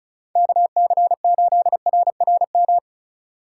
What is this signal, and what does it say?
Morse code, heard as a keyed tone, about 36 wpm, 700 Hz. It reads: KC8RRM
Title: USA Call Signs - 6 Characters - 3x Character Spacing - 35wpm